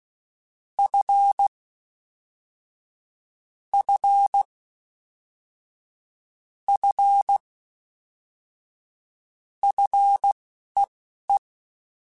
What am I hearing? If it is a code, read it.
Morse code, heard as a keyed tone, about 16 wpm, spaced as for 11 wpm, 800 Hz. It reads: F F F FEE
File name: F and L / F and L mp3 / F and L Prefixes 3x2 Twice @10